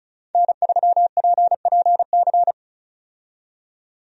Morse code, heard as a keyed tone, about 36 wpm, 700 Hz. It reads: N3PPC
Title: USA Call Signs - 5 Characters - 35wpm